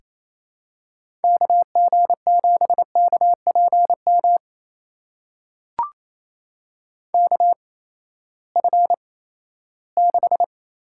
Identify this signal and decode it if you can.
Morse code, heard as a keyed tone, about 28 wpm, spaced as for 16 wpm, 700 Hz. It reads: KG7KPM KF6
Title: USA Call Signs - 6 Characters - 8x Character Spacing - 28wpm